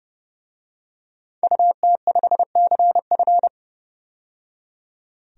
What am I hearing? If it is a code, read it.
Morse code, heard as a keyed tone, about 31 wpm, 700 Hz. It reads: UT5CF
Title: DXCC-100 Call Signs - 5 Characters - 30wpm